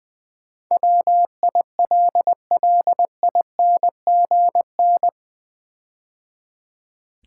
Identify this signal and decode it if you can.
Morse code, heard as a keyed tone, about 20 wpm, 700 Hz. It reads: WILLINGN
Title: Build A Word - 4 Letters At A Time - 11-Letter Words - 20wpm